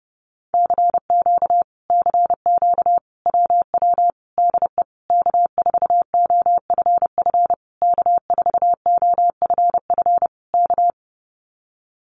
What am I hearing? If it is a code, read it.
Morse code, heard as a keyed tone, about 30 wpm, 700 Hz. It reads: CQ CQ WW DE K4OFF K4OFF K
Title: Common QSO Phrases Encore 30wpm